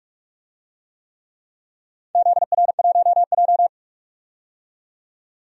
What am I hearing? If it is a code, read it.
Morse code, heard as a keyed tone, about 46 wpm, 700 Hz. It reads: ZR1J
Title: DXCC-100 Call Signs - 4 Characters - 45wpm